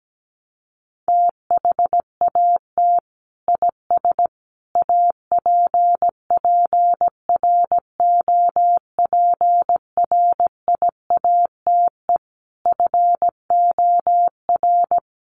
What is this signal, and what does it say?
Morse code, heard as a keyed tone, about 17 wpm, 700 Hz. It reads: THAT IS APPROPRIATE FOR
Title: Sentences from Top 900 Words 17wpm